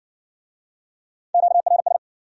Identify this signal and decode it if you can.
Morse code, heard as a keyed tone, about 58 wpm, 700 Hz. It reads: QRS